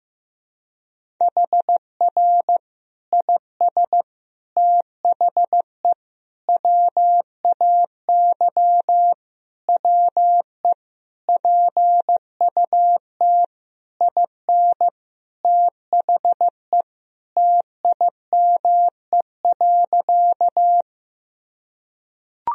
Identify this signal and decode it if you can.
Morse code, heard as a keyed tone, about 15 wpm, 700 Hz. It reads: HR IS THE WAY WE PUT IN THE TIME.
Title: Sentences from Top 200 Words and Abbreviations 15wpm